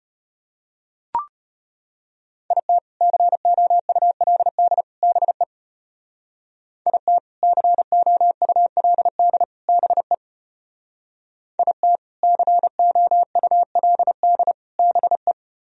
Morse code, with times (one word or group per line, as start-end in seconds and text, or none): 2.5-2.8 IT
3.0-4.8 COULD
5.0-5.4 BE
6.9-7.2 IT
7.4-9.4 COULD
9.7-10.1 BE
11.6-12.0 IT
12.2-14.5 COULD
14.8-15.3 BE